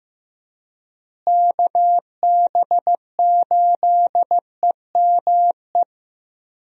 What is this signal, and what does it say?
Morse code, heard as a keyed tone, about 15 wpm, 700 Hz. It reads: KB8EME
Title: USA Call Signs - 6 Characters - Repeated 3x - 15wpm